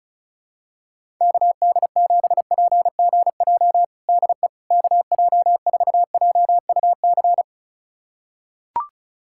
Morse code, heard as a keyed tone, about 36 wpm, 700 Hz. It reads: KD7PGJ DE KJ4JUC
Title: USA Callsign DE Callsign 35wpm